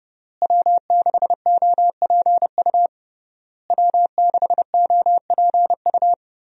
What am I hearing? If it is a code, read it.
Morse code, heard as a keyed tone, about 31 wpm, 700 Hz. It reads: W6OPU W6OPU
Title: USA Call Signs - 5 Characters - Repeated 3x - 30wpm